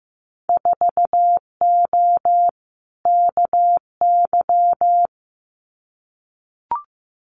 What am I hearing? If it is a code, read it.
Morse code, heard as a keyed tone, about 15 wpm, 700 Hz. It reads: 4O KY
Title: Winter Field Day Exchanges Encore 15wpm